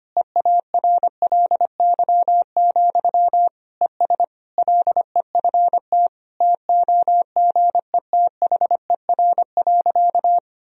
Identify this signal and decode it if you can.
Morse code, heard as a keyed tone, about 25 wpm, 700 Hz. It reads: EARLY, ES LEFT TOGETHER.
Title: Sentences from Top 400 Words and Abbreviations 25wpm